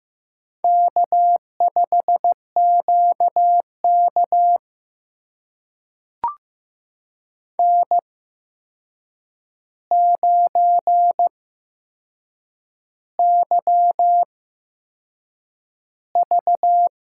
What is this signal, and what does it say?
Morse code, heard as a keyed tone, about 15 wpm, spaced as for 6 wpm, 700 Hz. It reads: K5QK N9YV
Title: USA Call Signs - 8x Character Spacing - 15wpm